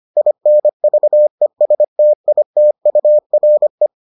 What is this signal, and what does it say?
Morse code, heard as a keyed tone, about 25 wpm, 600 Hz. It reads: INVESTITURE